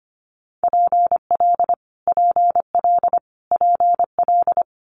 Morse code, heard as a keyed tone, about 25 wpm, 700 Hz. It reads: PL PL PL
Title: Multi-letter Phonemes Repeated 3x 25wpm